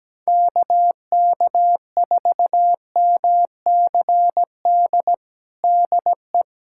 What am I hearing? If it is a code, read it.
Morse code, heard as a keyed tone, about 17 wpm, 700 Hz. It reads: KK4MCD DE
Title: USA Callsign DE Callsign - 6 Characters - 17wpm